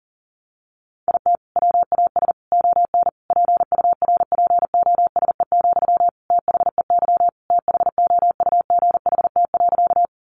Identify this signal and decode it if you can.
Morse code, heard as a keyed tone, about 40 wpm, 700 Hz. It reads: IT WAS ON PURPOSE, THEY THOUGHT.
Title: Sentences from Top 800 Words 40wpm